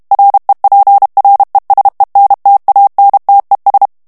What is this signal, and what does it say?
Morse code, heard as a keyed tone, about 32 wpm, 800 Hz. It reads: REPRESENTANTES
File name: morse mp3 / ESPAGNOL TEXTE EN CLAIR 1 VITESSE 30 MOTS MINUTES